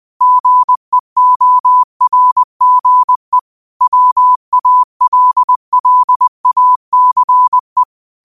Morse code, heard as a keyed tone, about 20 wpm, 1000 Hz. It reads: GEORGE WALLACE
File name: 20wpm